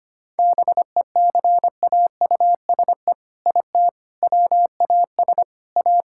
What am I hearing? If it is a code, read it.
Morse code, heard as a keyed tone, about 25 wpm, 700 Hz. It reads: BECAUSE IT WAS A
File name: practice / 25wpm / All-Age Appropriate Jokes - 25wpm